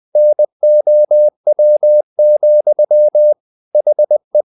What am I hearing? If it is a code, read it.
Morse code, heard as a keyed tone, about 20 wpm, 600 Hz. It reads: NOW, HE
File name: the old man and the sea part 2 20wpm